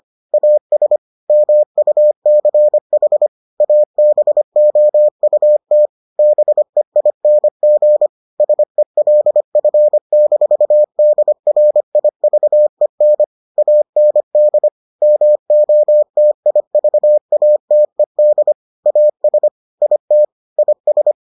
Morse code, heard as a keyed tone, about 25 wpm, 600 Hz. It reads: AS MUCH ABOUT BEING SELF-DRIVEN AND MOTIVATED AS IT IS